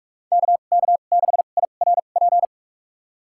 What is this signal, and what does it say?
Morse code, heard as a keyed tone, about 41 wpm, spaced as for 31 wpm, 700 Hz. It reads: KK6IRP
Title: USA Call Signs - 6 Characters - 2x Character Spacing - 45wpm